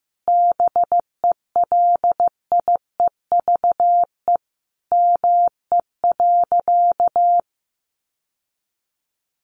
Morse code, heard as a keyed tone, about 15 wpm, 700 Hz. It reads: BELIEVE ME.
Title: Sentences from Top 1000 Words 15wpm